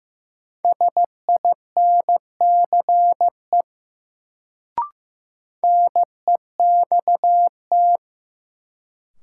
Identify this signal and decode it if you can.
Morse code, heard as a keyed tone, about 15 wpm, 700 Hz. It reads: SINCE NEXT
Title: Sentences from Top 300 Words - New Word Drill - 15wpm